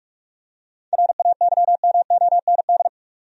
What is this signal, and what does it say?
Morse code, heard as a keyed tone, about 44 wpm, 700 Hz. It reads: RAYMOND
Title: USA Common Names - 45wpm